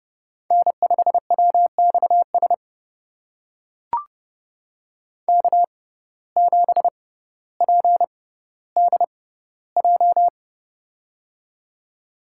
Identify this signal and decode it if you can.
Morse code, heard as a keyed tone, about 30 wpm, spaced as for 15 wpm, 700 Hz. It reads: N5WXS K7PDJ
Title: USA Call Signs - 5 Characters - 6x Character Spacing - 30wpm